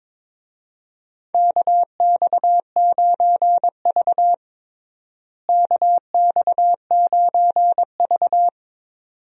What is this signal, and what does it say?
Morse code, heard as a keyed tone, about 22 wpm, 700 Hz. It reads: KX9V KX9V
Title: USA Call Signs - 4 Characters - Repeated 2x - 22wpm